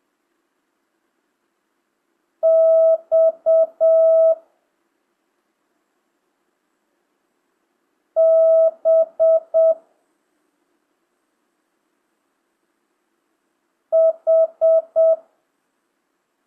Morse code, heard as a keyed tone, about 7 wpm, 650 Hz. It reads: X B H
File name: jh8cbh / 2025cw